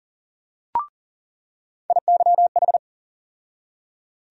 Morse code, heard as a keyed tone, about 40 wpm, 700 Hz. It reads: IYH
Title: USA Call Sign Postfixes - 3 Letters 40wpm